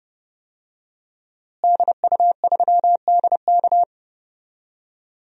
Morse code, heard as a keyed tone, about 32 wpm, 700 Hz. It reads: DU3DK
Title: DXCC-100 Call Signs - 5 Characters - Repeated 3x - 30wpm